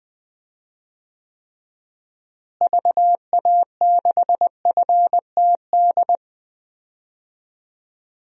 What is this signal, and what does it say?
Morse code, heard as a keyed tone, about 21 wpm, 700 Hz. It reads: VA6FTD
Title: DXCC-100 Call Signs - 6 Characters - Repeated 3x - 20wpm